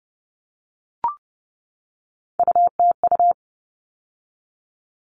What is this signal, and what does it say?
Morse code, heard as a keyed tone, about 30 wpm, 700 Hz. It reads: UTU
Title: All Cut Numbers - 3-Digit 30wpm